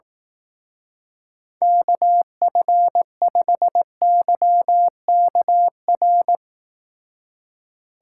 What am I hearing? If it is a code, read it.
Morse code, heard as a keyed tone, about 18 wpm, 700 Hz. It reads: KF5YKR
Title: Speed Racing - USA 2x3 Call Signs 15wpm